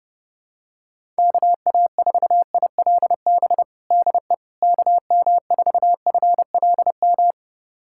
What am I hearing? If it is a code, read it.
Morse code, heard as a keyed tone, about 30 wpm, 700 Hz. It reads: KA4ILB DE KM4FLM
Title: USA Callsign DE Callsign 30wpm